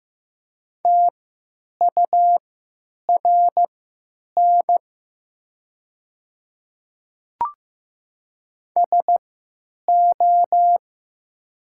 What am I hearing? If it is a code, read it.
Morse code, heard as a keyed tone, about 15 wpm, spaced as for 9 wpm, 700 Hz. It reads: TURN SO
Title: 3x Character Spacing - Top 200 Words - 15wpm